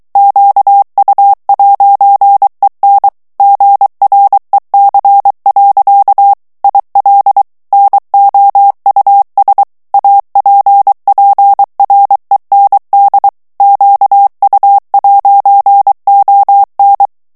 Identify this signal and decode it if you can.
Morse code, heard as a keyed tone, about 23 wpm, 800 Hz. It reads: QU'EN GREC. IL NOUS APPREND QU'ON
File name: morse mp3 / FRANCAIS TEXTE CLAIR 2 VITESSE 22 MOTS MINUTES